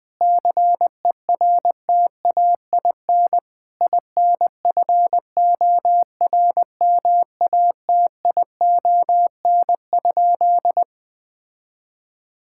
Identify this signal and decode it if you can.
Morse code, heard as a keyed tone, about 21 wpm, 700 Hz. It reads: CERTAIN INFORMATION?
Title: Sentences from Top 1000 Words - Review Part 6 - 20wpm